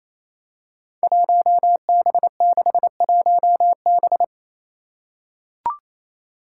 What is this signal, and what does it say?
Morse code, heard as a keyed tone, about 28 wpm, 700 Hz. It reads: 1B61B
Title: Five Challenging Letter-Number Combinations 28wpm